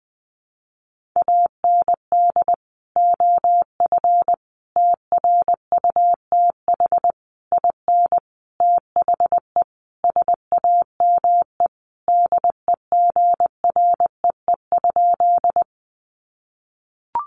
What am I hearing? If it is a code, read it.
Morse code, heard as a keyed tone, about 20 wpm, 700 Hz. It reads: AND OF TRUTH IN THE SAME DEGREE?
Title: Sentences from Top 1500 Words 20wpm